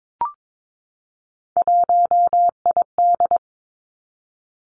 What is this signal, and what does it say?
Morse code, heard as a keyed tone, about 22 wpm, 700 Hz. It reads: 1ID
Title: Three Challenging Letter-Number Combinations 22wpm